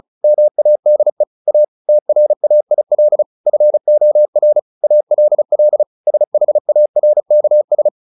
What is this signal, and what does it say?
Morse code, heard as a keyed tone, about 34 wpm, 600 Hz. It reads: MADE A TRAIL FOR ALL SHARKS
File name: the old man and the sea 2 35wpm